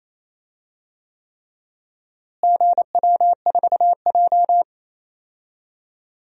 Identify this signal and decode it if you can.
Morse code, heard as a keyed tone, about 28 wpm, 700 Hz. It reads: GW4J
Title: DXCC-100 Call Signs - 4 Characters - Repeated 3x - 28wpm